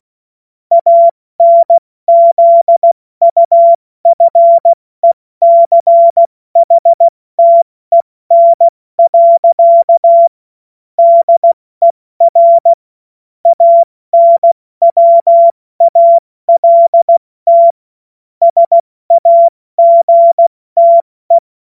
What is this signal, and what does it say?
Morse code, heard as a keyed tone, about 16 wpm, 680 Hz. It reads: ANZUFECHTEN. DER ANWALT SAGTE
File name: Morsekurs / Witze / Witz 30. August 2022 70 F80